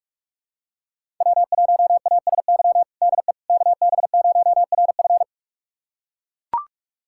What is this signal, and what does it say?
Morse code, heard as a keyed tone, about 41 wpm, 700 Hz. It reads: W1ASY DE KB0RF
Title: USA Callsign DE Callsign - 5 Characters - 45wpm